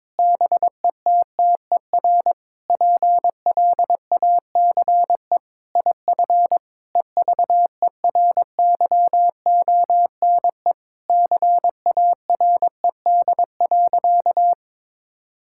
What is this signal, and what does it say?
Morse code, heard as a keyed tone, about 22 wpm, 700 Hz. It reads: BETTER PLACE IF EVERYONE CARED.
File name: practice / 22wpm / Sentences with 8-Word Phrases - 22wpm